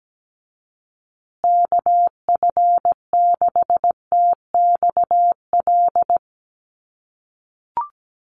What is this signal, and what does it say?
Morse code, heard as a keyed tone, about 17 wpm, 700 Hz. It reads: KF6TXL
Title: USA Call Signs Repeated 3x 17wpm